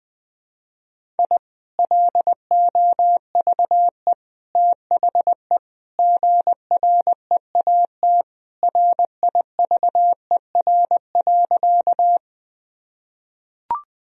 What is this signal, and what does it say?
Morse code, heard as a keyed tone, about 21 wpm, 700 Hz. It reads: I LOVE THE GREAT RIVER.
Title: Sentences from Top 1000 Words - Review Part 1 - 20wpm